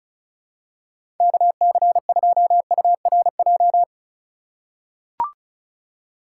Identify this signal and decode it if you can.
Morse code, heard as a keyed tone, about 36 wpm, 700 Hz. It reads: KC2URJ
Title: USA Call Signs - 6 Characters - 3x Character Spacing - 35wpm